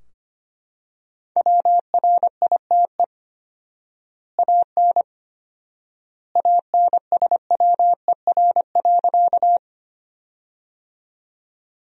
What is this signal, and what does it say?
Morse code, heard as a keyed tone, about 25 wpm, 700 Hz. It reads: WRITE AN ANSWER.
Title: 4x Word-Spacing - Sentences from Top 300 Words - 25wpm